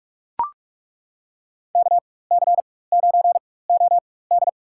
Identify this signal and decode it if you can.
Morse code, heard as a keyed tone, about 42 wpm, spaced as for 20 wpm, 700 Hz. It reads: KC9OD
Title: USA Call Signs - 4x Character Spacing - 45wpm